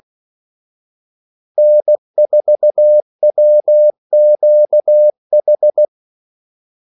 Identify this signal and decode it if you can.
Morse code, heard as a keyed tone, about 16 wpm, 600 Hz. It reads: N4WQH
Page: Kaldesignaler 16 wpm